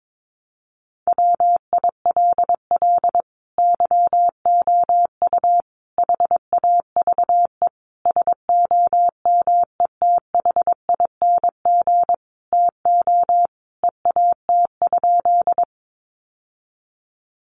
Morse code, heard as a keyed tone, about 22 wpm, 700 Hz. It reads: WILL YOU HAVE SOMETHING TO EAT?
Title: Sentences from Top 600 Words 22wpm